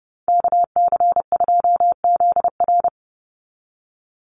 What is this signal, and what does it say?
Morse code, heard as a keyed tone, about 30 wpm, 700 Hz. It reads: KC2ZR
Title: USA Call Signs 30wpm